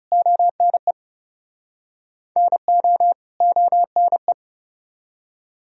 Morse code, heard as a keyed tone, about 40 wpm, 700 Hz. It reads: ONE NO ONE